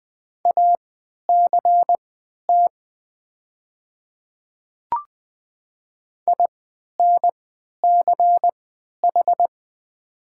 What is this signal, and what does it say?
Morse code, heard as a keyed tone, about 20 wpm, spaced as for 11 wpm, 690 Hz. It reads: ACT INCH
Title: 3x Character Spacing - Top 500 Words - 20wpm